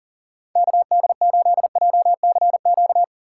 Morse code, heard as a keyed tone, about 40 wpm, 700 Hz. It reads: KD8JCQ
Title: USA Call Signs - 6 Characters - 2x Character Spacing - 40wpm